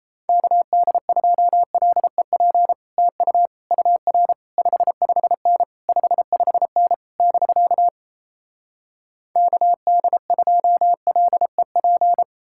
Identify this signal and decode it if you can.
Morse code, heard as a keyed tone, about 38 wpm, 700 Hz. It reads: KD2LEP TU UR 55N 55N <BK> KD2LEP